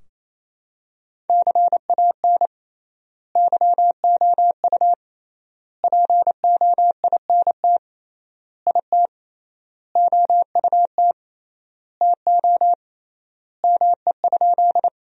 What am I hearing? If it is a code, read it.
Morse code, heard as a keyed tone, about 27 wpm, 700 Hz. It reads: CAN YOU POINT IT OUT TO ME?
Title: 3x Word-Spacing - Sentences from Top 200 Words - 28wpm